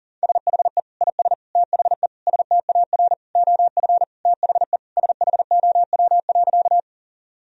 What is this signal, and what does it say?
Morse code, heard as a keyed tone, about 37 wpm, 700 Hz. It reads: SHE IS THE STAR OF THE SHOW.
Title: Sentences from Top 1000 Words - Review Part 2 - 40wpm